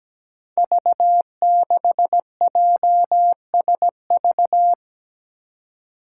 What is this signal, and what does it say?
Morse code, heard as a keyed tone, about 17 wpm, 700 Hz. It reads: V6JSV
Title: Five Challenging Letter-Number Combinations 17wpm